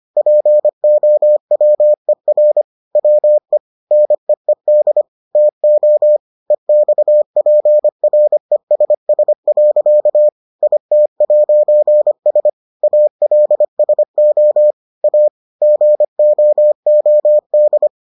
Morse code, heard as a keyed tone, about 25 wpm, 600 Hz. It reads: POWER WE NEED TO EXPRESS. IT'S ALSO A GOOD